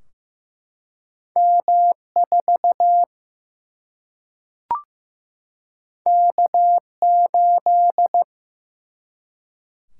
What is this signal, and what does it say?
Morse code, heard as a keyed tone, about 15 wpm, 700 Hz. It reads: M4 K8
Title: Single-Letter Single-Number 15wpm